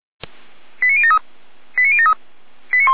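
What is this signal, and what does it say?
Five-tone selective call (ZVEI), heard as quick runs of five tones, tones of about 70 ms. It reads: 89073 89073
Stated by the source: ZVEI five-tone selective call received on an SDR.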